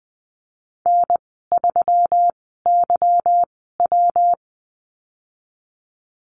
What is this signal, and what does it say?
Morse code, heard as a keyed tone, about 20 wpm, spaced as for 15 wpm, 700 Hz. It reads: N3YW
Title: 154 Lesson 25g - Letter p - Call Signs 20wpm